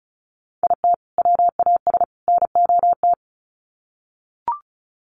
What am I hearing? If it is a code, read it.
Morse code, heard as a keyed tone, about 34 wpm, 700 Hz. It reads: IT WAS NOT
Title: Sets of 3 Words 35wpm